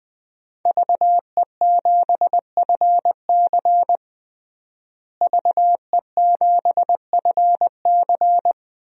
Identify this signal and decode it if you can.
Morse code, heard as a keyed tone, about 20 wpm, 700 Hz. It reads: VE7FC VE7FC
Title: DXCC-100 Call Signs - 5 Characters - Repeated 2x - 20wpm